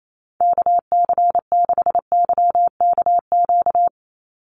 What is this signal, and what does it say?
Morse code, heard as a keyed tone, about 28 wpm, 700 Hz. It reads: KC6YKQ
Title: USA Call Signs 28wpm